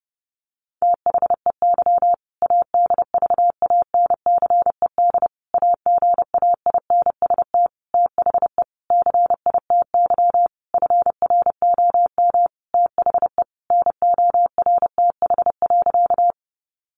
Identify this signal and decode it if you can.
Morse code, heard as a keyed tone, about 30 wpm, 700 Hz. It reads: THEY ADVANCED AGAINST THE CITY FROM THE NORTH.
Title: Sentences from Top 1200 Words 30wpm